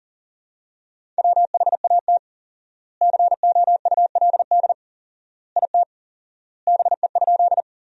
Morse code, heard as a keyed tone, about 40 wpm, 700 Hz. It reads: WHAT COULD IT BE?
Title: 4x Word-Spacing - Sentences from Top 100 Words - 40wpm